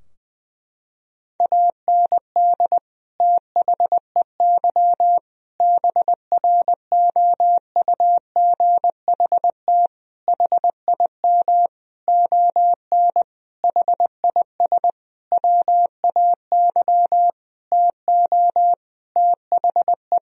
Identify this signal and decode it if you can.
Morse code, heard as a keyed tone, about 20 wpm, 700 Hz. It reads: AND THEY BROUGHT HIM ON HIS WAY TO THE
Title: Sentences from Top 1000 Words - Review Part 2 - 20wpm